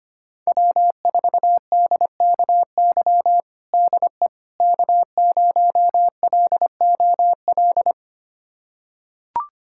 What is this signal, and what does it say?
Morse code, heard as a keyed tone, about 25 wpm, 700 Hz. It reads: W4DKY DE K0LOL
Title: USA Callsign DE Callsign - 5 Characters - 25wpm